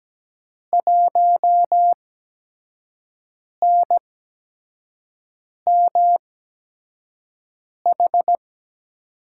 Morse code, heard as a keyed tone, about 17 wpm, 700 Hz. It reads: 1 N M H
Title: USA Call Signs - 5 Characters - 8x Character Spacing - 17wpm